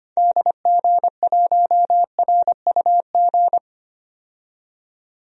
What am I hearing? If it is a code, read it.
Morse code, heard as a keyed tone, about 25 wpm, 700 Hz. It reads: DG1RUG
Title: DXCC-100 Call Signs - 6 Characters - 25wpm